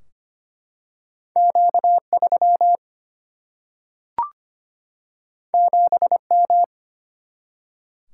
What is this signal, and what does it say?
Morse code, heard as a keyed tone, about 25 wpm, 700 Hz. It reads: Q3 7M
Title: Single-Letter Single-Number 25wpm